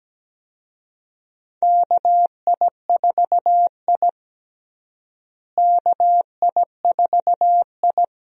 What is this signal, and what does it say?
Morse code, heard as a keyed tone, about 17 wpm, 700 Hz. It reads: KI4I KI4I
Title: USA Call Signs - 4 Characters - Repeated 2x - 17wpm